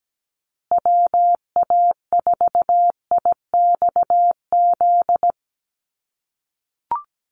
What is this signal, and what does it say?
Morse code, heard as a keyed tone, about 17 wpm, 700 Hz. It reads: WA4IXZ
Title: USA Call Signs Repeated 3x 17wpm